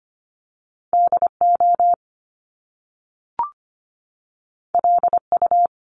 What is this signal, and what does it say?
Morse code, heard as a keyed tone, about 25 wpm, 700 Hz. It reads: DO LU
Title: Top Two Letter Combinations in Words 25wpm